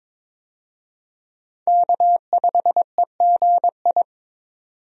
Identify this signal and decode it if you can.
Morse code, heard as a keyed tone, about 22 wpm, 700 Hz. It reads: K5EGI
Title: USA Call Signs - 5 Characters - 22wpm